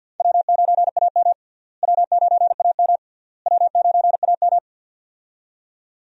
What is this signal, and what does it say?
Morse code, heard as a keyed tone, about 52 wpm, 700 Hz. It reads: W9AM W9AM W9AM
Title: USA Call Signs - 4 Characters - Repeated 3x - 50wpm